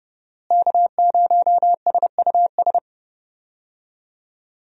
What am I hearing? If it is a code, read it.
Morse code, heard as a keyed tone, about 30 wpm, 700 Hz. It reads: K0SUS
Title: USA Call Signs - 5 Characters - Repeated 2x - 30wpm